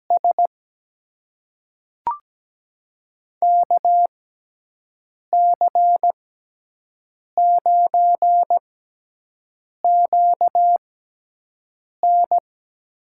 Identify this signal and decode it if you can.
Morse code, heard as a keyed tone, about 17 wpm, spaced as for 6 wpm, 700 Hz. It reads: S KC9QN
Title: USA Call Signs - 6 Characters - 6x Character Spacing - 17wpm